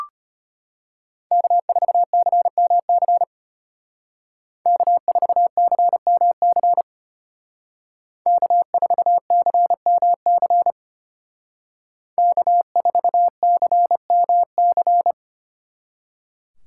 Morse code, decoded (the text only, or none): K4CMC K4CMC K4CMC K4CMC